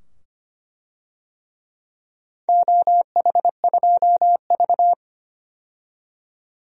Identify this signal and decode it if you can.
Morse code, heard as a keyed tone, about 25 wpm, 700 Hz. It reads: OH2V
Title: DXCC-100 Call Signs - 4 Characters - 25wpm